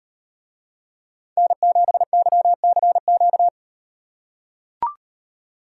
Morse code, heard as a keyed tone, about 35 wpm, 700 Hz. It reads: N7YCQ